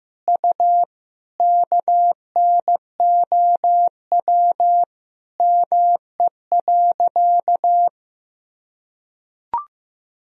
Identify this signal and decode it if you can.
Morse code, heard as a keyed tone, about 15 wpm, 700 Hz. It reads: U KNOW ME.
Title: Sentences from Top 200 Words and Abbreviations 15wpm